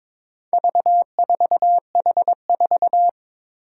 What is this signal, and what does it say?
Morse code, heard as a keyed tone, about 23 wpm, 700 Hz. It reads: V4H4
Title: Four Challenging Letter-Number Combinations 22wpm